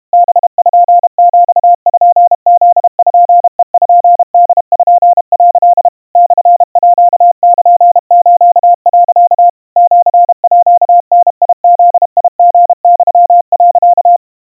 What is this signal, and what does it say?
Morse code (Japanese Wabun code), heard as a keyed tone, about 32 wpm, 700 Hz. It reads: パピプペポ」 モールス、 シーダブリユ、
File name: 110 cpm 700 Hz Crispy